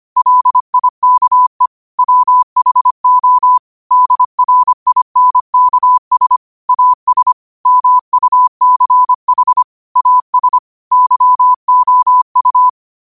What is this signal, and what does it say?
Morse code, heard as a keyed tone, about 25 wpm, 1000 Hz. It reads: LIKE WHO DRINKS AS MUCH AS YOU
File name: qotdfiles / 25wpm